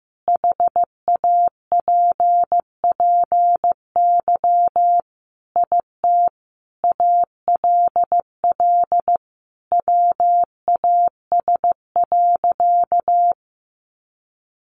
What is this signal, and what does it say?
Morse code, heard as a keyed tone, about 15 wpm, 700 Hz. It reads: HAPPY IT ALL WAS.
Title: Sentences from Top 1000 Words 15wpm